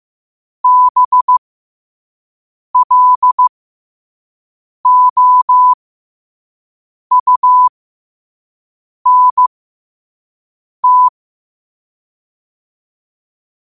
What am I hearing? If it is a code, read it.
Morse code, heard as a keyed tone, about 15 wpm, spaced as for 5 wpm, 1000 Hz. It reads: BLOUNT
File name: qotdfiles / 5wpm